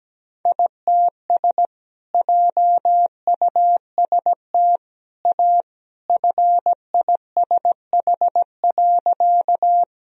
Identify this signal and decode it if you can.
Morse code, heard as a keyed tone, about 17 wpm, 700 Hz. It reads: ITS JUST A FISH.
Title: Sentences from Top 1000 Words - Review Part 1 - 17wpm